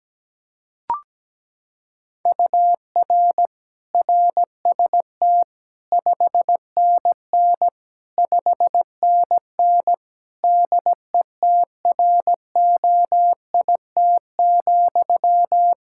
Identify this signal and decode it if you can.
Morse code, heard as a keyed tone, about 17 wpm, 700 Hz. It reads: UR RST 5NN 5NN DETROIT,
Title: Practice QSO - Exchange 1 - Beginning - 17wpm